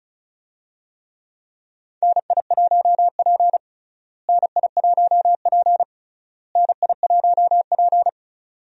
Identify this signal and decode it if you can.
Morse code, heard as a keyed tone, about 36 wpm, 700 Hz. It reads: NI1P NI1P NI1P
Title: USA Call Signs - 4 Characters - Repeated 3x - 35wpm